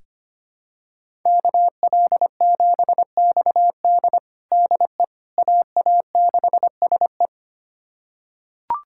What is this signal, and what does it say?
Morse code, heard as a keyed tone, about 25 wpm, 700 Hz. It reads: KL7XD DE AA6SE
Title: USA Callsign DE Callsign - 5 Characters - 25wpm